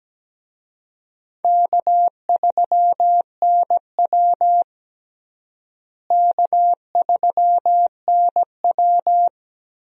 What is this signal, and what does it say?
Morse code, heard as a keyed tone, about 17 wpm, 700 Hz. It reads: K3NW K3NW
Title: USA Call Signs - 4 Characters - Repeated 2x - 17wpm